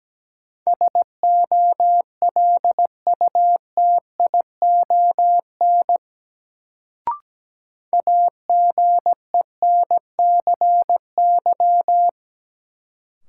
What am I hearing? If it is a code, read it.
Morse code, heard as a keyed tone, about 17 wpm, 700 Hz. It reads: SOLUTION AGENCY
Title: Sentences from Top 1300 Words - New Word Drill - 17wpm